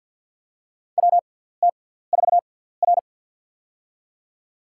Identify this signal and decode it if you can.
Morse code, heard as a keyed tone, about 52 wpm, spaced as for 17 wpm, 700 Hz. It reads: WT4R
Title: USA Call Signs - 4 Characters - 6x Character Spacing - 50wpm